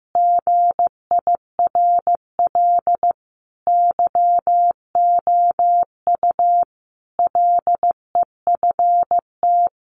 Morse code, heard as a keyed tone, about 15 wpm, 700 Hz. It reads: GIRL YOU LEFT
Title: Sentences from Top 1500 Words 15wpm